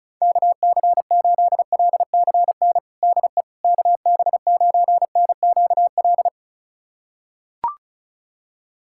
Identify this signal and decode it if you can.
Morse code, heard as a keyed tone, about 36 wpm, 700 Hz. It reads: KC8LCN DE KB9NQL
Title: USA Callsign DE Callsign - 6 Characters - 35wpm